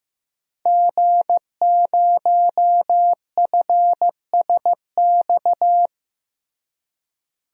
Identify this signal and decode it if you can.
Morse code, heard as a keyed tone, about 15 wpm, 700 Hz. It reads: G0FSX
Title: DXCC-100 Call Signs - 5 Characters - Repeated 3x - 15wpm